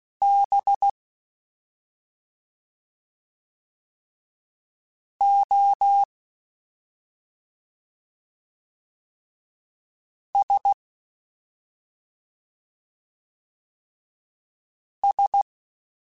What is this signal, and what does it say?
Morse code, heard as a keyed tone, about 16 wpm, 800 Hz. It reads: B O S S
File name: Special INGs / Special INGs-mp3s / SlowToFaster ing @5